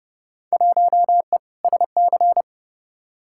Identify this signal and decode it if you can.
Morse code, heard as a keyed tone, about 30 wpm, 700 Hz. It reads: 1E SC
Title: Field Day Exchanges - 30wpm